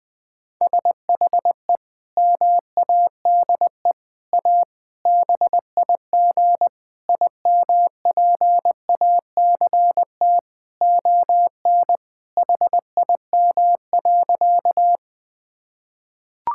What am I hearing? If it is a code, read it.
Morse code, heard as a keyed tone, about 20 wpm, 700 Hz. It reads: SHE MADE A BIG IMPACT ON HIM.
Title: Sentences from Top 1000 Words - Review Part 6 - 20wpm